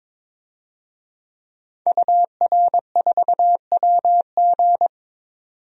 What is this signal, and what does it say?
Morse code, heard as a keyed tone, about 22 wpm, 700 Hz. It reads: UR4WG
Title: DXCC-100 Call Signs - 5 Characters - 22wpm